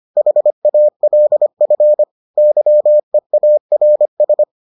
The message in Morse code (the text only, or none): HALF YEARS